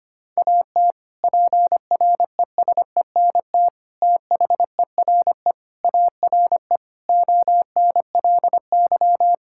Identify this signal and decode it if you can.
Morse code, heard as a keyed tone, about 25 wpm, 700 Hz. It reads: AT PRESENT THERE ARE ONLY
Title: Sentences from Top 1000 Words - Review Part 5 - 25wpm